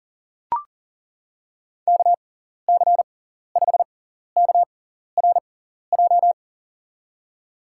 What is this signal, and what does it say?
Morse code, heard as a keyed tone, about 40 wpm, spaced as for 14 wpm, 700 Hz. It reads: KC5KRJ
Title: USA Call Signs - 6 Characters - 6x Character Spacing - 40wpm